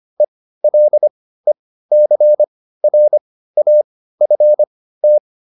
Morse code, heard as a keyed tone, about 25 wpm, spaced as for 15 wpm, 600 Hz. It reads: ELECRAFT